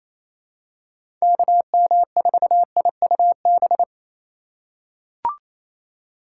Morse code, heard as a keyed tone, about 28 wpm, 700 Hz. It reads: KM4IUB
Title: USA Call Signs - 6 Characters - 4x Character Spacing - 28wpm